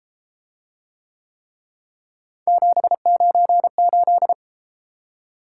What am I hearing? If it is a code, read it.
Morse code, heard as a keyed tone, about 29 wpm, 700 Hz. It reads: 798